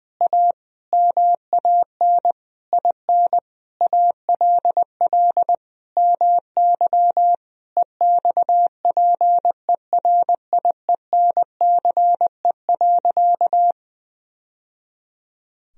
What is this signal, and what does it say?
Morse code, heard as a keyed tone, about 20 wpm, 700 Hz. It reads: A MAN IN ALL MY EXPERIENCE.
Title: Sentences from Top 1000 Words - Review Part 3 - 20wpm